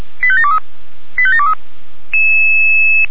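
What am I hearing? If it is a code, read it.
Five-tone selective call (ZVEI), heard as quick runs of five tones, tones of about 70 ms. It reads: 85723 85723
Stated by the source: Five-tone ZVEI selective call received on an SDR.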